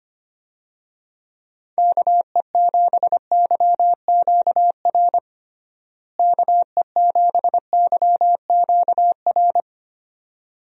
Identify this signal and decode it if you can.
Morse code, heard as a keyed tone, about 25 wpm, 700 Hz. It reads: KE7YQR KE7YQR
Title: USA Call Signs - 6 Characters - Repeated 2x - 25wpm